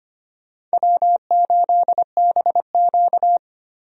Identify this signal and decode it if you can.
Morse code, heard as a keyed tone, about 25 wpm, 700 Hz. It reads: W8BQ
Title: USA Call Signs - 4 Characters - 25wpm